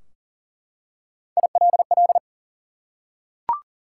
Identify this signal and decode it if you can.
Morse code, heard as a keyed tone, about 44 wpm, 700 Hz. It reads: ILL
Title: Three Letter Words - 40wpm